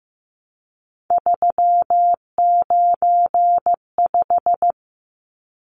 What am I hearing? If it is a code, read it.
Morse code, heard as a keyed tone, about 15 wpm, 700 Hz. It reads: 395
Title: Numbers - 3-Digit 15wpm